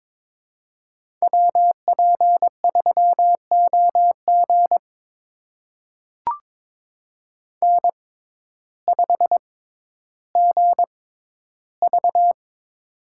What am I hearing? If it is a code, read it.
Morse code, heard as a keyed tone, about 21 wpm, spaced as for 12 wpm, 700 Hz. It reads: WP3OG N5GV
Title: USA Call Signs - 6x Character Spacing - 22wpm